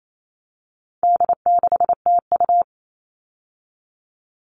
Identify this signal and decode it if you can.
Morse code, heard as a keyed tone, about 28 wpm, 700 Hz. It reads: D6TU
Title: All Cut Numbers - 4-Digit 28wpm